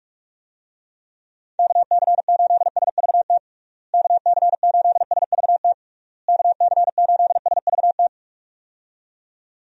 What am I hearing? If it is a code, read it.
Morse code, heard as a keyed tone, about 46 wpm, 700 Hz. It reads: KC8SVT KC8SVT KC8SVT
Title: USA Call Signs - 6 Characters - Repeated 3x - 45wpm